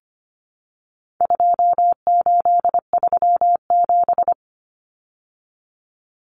Morse code, heard as a keyed tone, about 25 wpm, 700 Hz. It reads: 2837
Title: Common Cut Numbers - 4-Digit 25wpm